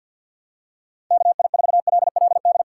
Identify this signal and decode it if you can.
Morse code, heard as a keyed tone, about 48 wpm, 700 Hz. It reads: KI4LLD
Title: USA Call Signs - 4x Character Spacing - 50wpm